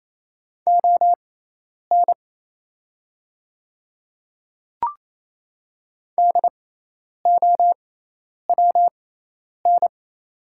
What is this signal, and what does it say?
Morse code, heard as a keyed tone, about 28 wpm, spaced as for 10 wpm, 700 Hz. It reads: ON DOWN
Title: 6x Character Spacing - Top 200 Words - 28wpm